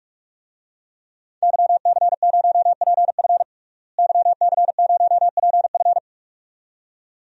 Morse code, heard as a keyed tone, about 41 wpm, 700 Hz. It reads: YC0PF YC0PF
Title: DXCC-100 Call Signs - 5 Characters - Repeated 2x - 45wpm